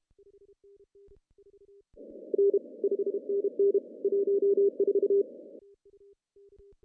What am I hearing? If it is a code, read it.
Morse code, heard as a keyed tone, about 30 wpm, 410 Hz. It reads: N 5NN 14
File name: CW / cqww-cw-k5tr